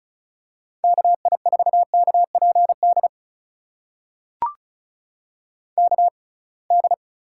Morse code, heard as a keyed tone, about 36 wpm, spaced as for 27 wpm, 700 Hz. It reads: KI4KPD KD
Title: USA Call Signs - 6 Characters - 6x Character Spacing - 35wpm